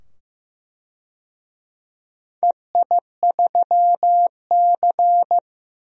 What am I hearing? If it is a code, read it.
Morse code, heard as a keyed tone, about 15 wpm, 700 Hz. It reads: EI3C
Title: DXCC-100 Call Signs - 4 Characters - 15wpm